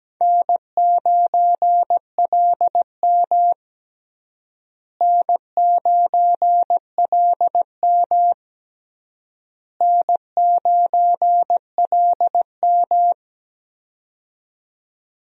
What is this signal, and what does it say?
Morse code, heard as a keyed tone, about 17 wpm, 700 Hz. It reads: N9LM N9LM N9LM
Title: USA Call Signs - 4 Characters - Repeated 3x - 17wpm